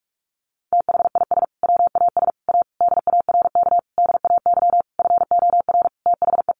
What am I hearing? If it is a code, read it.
Morse code, heard as a keyed tone, about 46 wpm, 700 Hz. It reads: THIS WAS A DARK DAY FOR THE
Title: Sentences from Top 1300 Words 45wpm